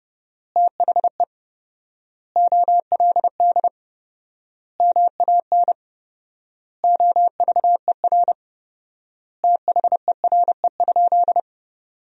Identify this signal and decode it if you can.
Morse code, heard as a keyed tone, about 30 wpm, 700 Hz. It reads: THE OLD MAN OVER THERE?
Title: 4x Word-Spacing - Sentences from Top 200 Words - 30wpm